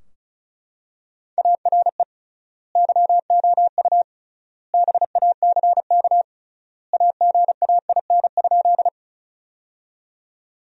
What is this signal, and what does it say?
Morse code, heard as a keyed tone, about 36 wpm, 700 Hz. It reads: ARE YOU BACK AGAIN?
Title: 3x Word-Spacing - Sentences from Top 200 Words - 35wpm